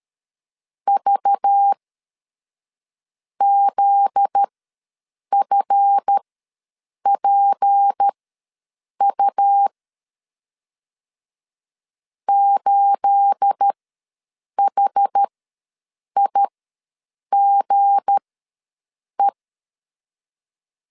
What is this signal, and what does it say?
Morse code, heard as a keyed tone, about 13 wpm, spaced as for 6 wpm, 800 Hz. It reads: VZFPU 8HIGE
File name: LEZIONI / 9 286 ws 12wpm 3351